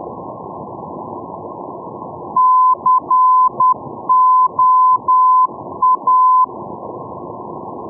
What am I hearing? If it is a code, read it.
Morse code, heard as a keyed tone, about 10 wpm, 1020 Hz. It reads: COA
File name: europe / e-601 ndb la coruna coa-401 np lacoruna 2016-05